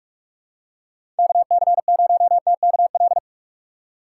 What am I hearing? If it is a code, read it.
Morse code, heard as a keyed tone, about 41 wpm, 700 Hz. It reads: KC0TKL
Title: USA Call Signs - 6 Characters - 6x Character Spacing - 45wpm